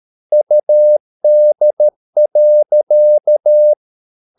Morse code, heard as a keyed tone, about 13 wpm, 600 Hz. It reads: UD.
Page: Ferie 13 wpm